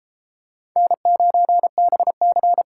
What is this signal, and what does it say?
Morse code, heard as a keyed tone, about 29 wpm, 700 Hz. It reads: N9BC